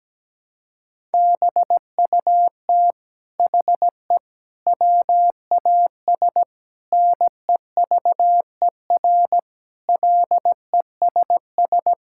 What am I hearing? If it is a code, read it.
Morse code, heard as a keyed tone, about 17 wpm, 700 Hz. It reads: BUT HE WAS NEVER LESS
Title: Sentences from Top 1000 Words - Review Part 5 - 17wpm